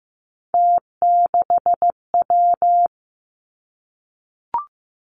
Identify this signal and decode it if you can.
Morse code, heard as a keyed tone, about 15 wpm, 700 Hz. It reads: T6W
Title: Three Challenging Letter-Number Combinations 15wpm